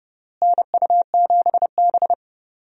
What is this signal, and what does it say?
Morse code, heard as a keyed tone, about 30 wpm, 700 Hz. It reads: NU7B
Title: USA Call Signs - 4 Characters - Repeated 3x - 30wpm